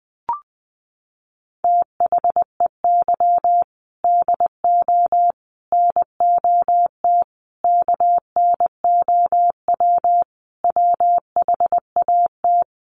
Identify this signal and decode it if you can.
Morse code, heard as a keyed tone, about 20 wpm, 700 Hz. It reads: THEY DO NOT KNOW WHAT
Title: Sets of 5 Words Encore 20wpm